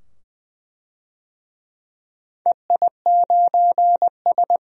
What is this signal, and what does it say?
Morse code, heard as a keyed tone, about 20 wpm, 700 Hz. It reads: EI9S
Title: DXCC-100 Call Signs - 4 Characters - 20wpm